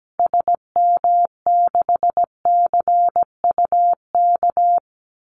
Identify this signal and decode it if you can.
Morse code, heard as a keyed tone, about 17 wpm, 700 Hz. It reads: SM6CUK
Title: Call Signs Repeated 2x - 17wpm